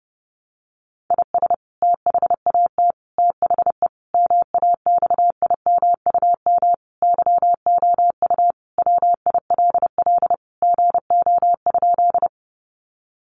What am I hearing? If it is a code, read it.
Morse code, heard as a keyed tone, about 30 wpm, 700 Hz. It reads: IS THAT THE MAXIMUM YOU WILL GO?
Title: Sentences from Top 1300 Words 30wpm